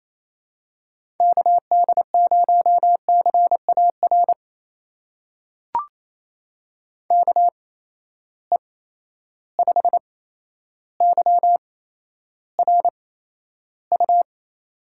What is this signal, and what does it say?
Morse code, heard as a keyed tone, about 28 wpm, spaced as for 12 wpm, 700 Hz. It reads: KD0CAR KE5YRU
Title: USA Call Signs - 6 Characters - 8x Character Spacing - 28wpm